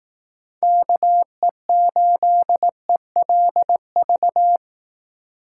Morse code, heard as a keyed tone, about 18 wpm, 700 Hz. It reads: KE8ELV
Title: Speed Racing - USA 2x3 Call Signs 15wpm